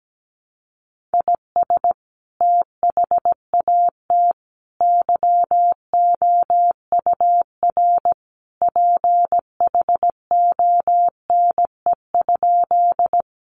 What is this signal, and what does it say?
Morse code, heard as a keyed tone, about 17 wpm, 700 Hz. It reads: IS THAT YOUR PHONE?
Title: Sentences from Top 1100 Words 17wpm